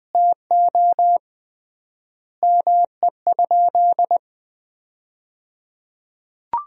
Morse code, heard as a keyed tone, about 21 wpm, 700 Hz. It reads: TO ME?
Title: 3x Word-Spacing - Sentences from Top 200 Words - 20wpm